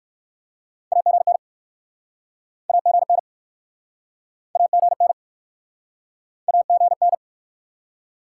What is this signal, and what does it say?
Morse code, heard as a keyed tone, about 67 wpm, 700 Hz. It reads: AGN AGN AGN AGN